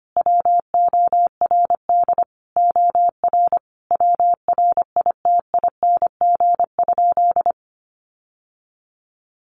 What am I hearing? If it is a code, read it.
Morse code, heard as a keyed tone, about 25 wpm, 700 Hz. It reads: WORD OR WRITING?
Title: Sentences from Top 1200 Words 25wpm